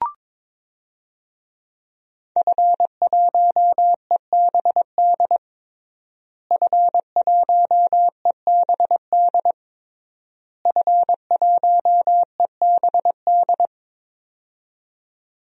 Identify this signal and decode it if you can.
Morse code, heard as a keyed tone, about 22 wpm, 700 Hz. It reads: F1EBD F1EBD F1EBD
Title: DXCC-100 Call Signs - 5 Characters - Repeated 3x - 22wpm